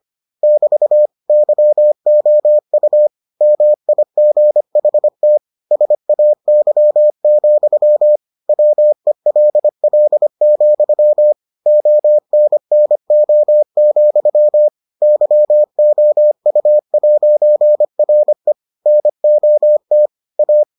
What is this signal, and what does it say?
Morse code, heard as a keyed tone, about 25 wpm, 600 Hz. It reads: <BT> YOU MIGHT SAY, WELL, ONNO, YOU'RE NOT A